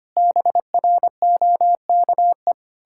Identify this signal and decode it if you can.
Morse code, heard as a keyed tone, about 25 wpm, 700 Hz. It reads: BROKE
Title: Pangram Challenge v2 - 25wpm